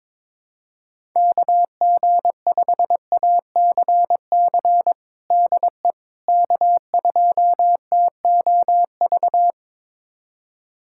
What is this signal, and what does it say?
Morse code, heard as a keyed tone, about 22 wpm, 700 Hz. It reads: KG5ACC DE K2TOV
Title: USA Callsign DE Callsign 22wpm